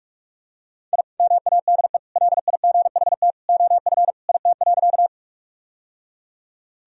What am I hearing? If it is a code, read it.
Morse code, heard as a keyed tone, about 41 wpm, 700 Hz. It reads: I MADE LIGHT OF IT.
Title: Sentences from Top 1000 Words - Review Part 1 - 45wpm